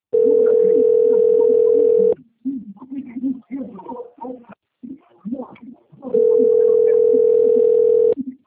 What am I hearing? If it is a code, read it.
Ringback tone (North America), 440 + 480 Hz.